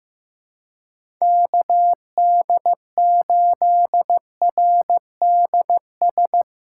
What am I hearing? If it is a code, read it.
Morse code, heard as a keyed tone, about 17 wpm, 700 Hz. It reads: KD8RDS
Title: Speed Racing - USA 2x3 Call Signs 15wpm